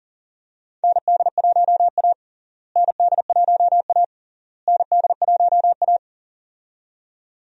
Morse code, heard as a keyed tone, about 40 wpm, 700 Hz. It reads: ND1A ND1A ND1A
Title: USA Call Signs - 4 Characters - Repeated 3x - 40wpm